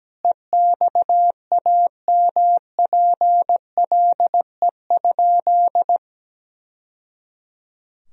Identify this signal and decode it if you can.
Morse code, heard as a keyed tone, about 17 wpm, 700 Hz. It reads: EXAMPLE?
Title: Sentences from Top 1000 Words - Review Part 1 - 17wpm